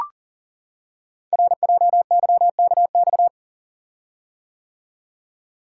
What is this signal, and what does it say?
Morse code, heard as a keyed tone, about 42 wpm, 690 Hz. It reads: RJYKX
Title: Code Groups - 5 Letters - 40wpm